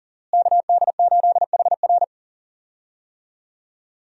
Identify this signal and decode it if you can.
Morse code, heard as a keyed tone, about 40 wpm, 700 Hz. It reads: KD8HR
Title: USA Call Signs - 5 Characters - Repeated 3x - 40wpm